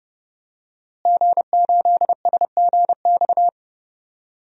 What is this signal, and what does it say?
Morse code, heard as a keyed tone, about 30 wpm, 700 Hz. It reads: G8SGX
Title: DXCC-100 Call Signs - 5 Characters - Repeated 3x - 30wpm